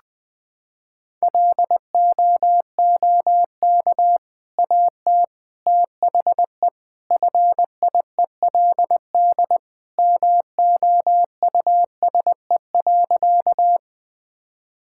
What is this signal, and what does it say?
Morse code, heard as a keyed tone, about 21 wpm, 700 Hz. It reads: LOOK AT THE FIELD MOUSE.
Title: Sentences from Top 1000 Words - Review Part 5 - 20wpm